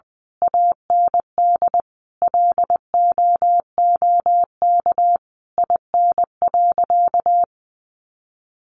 Morse code, heard as a keyed tone, about 20 wpm, 700 Hz. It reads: AND LOOK IN.
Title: Sentences from Top 1300 Words 20wpm